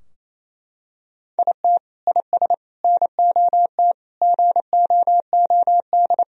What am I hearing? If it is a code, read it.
Morse code, heard as a keyed tone, about 28 wpm, 700 Hz. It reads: IT IS NOT GOOD
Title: Sentences from Top 1000 Words - Review Part 2 - 28wpm